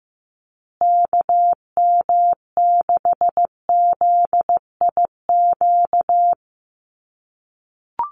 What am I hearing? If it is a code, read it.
Morse code, heard as a keyed tone, about 15 wpm, 700 Hz. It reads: KM6ZIQ
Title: USA Call Signs Repeated 2x 15wpm